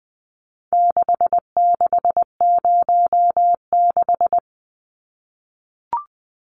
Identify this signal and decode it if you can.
Morse code, heard as a keyed tone, about 20 wpm, 700 Hz. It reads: 6606
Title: Numbers - 4-Digit 20wpm